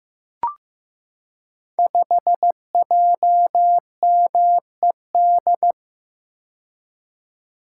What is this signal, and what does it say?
Morse code, heard as a keyed tone, about 15 wpm, 700 Hz. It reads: 5JMED
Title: Code Groups - 5 Letter:Numbers - 15wpm